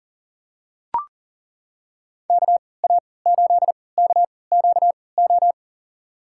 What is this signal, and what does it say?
Morse code, heard as a keyed tone, about 39 wpm, spaced as for 23 wpm, 700 Hz. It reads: KA8KQO
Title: USA Call Signs - 3x Character Spacing - 40wpm